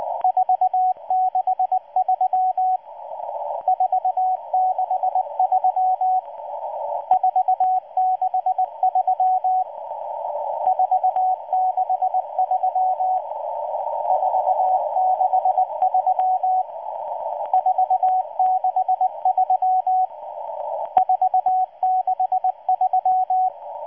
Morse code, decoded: 463 463 463 463 463 463 463